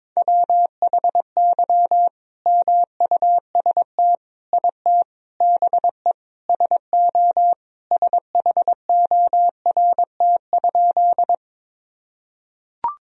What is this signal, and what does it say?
Morse code, heard as a keyed tone, about 22 wpm, 700 Hz. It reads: WHY MUST IT BE SO SHORT?
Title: Sentences from Top 1000 Words - Review Part 2 - 22wpm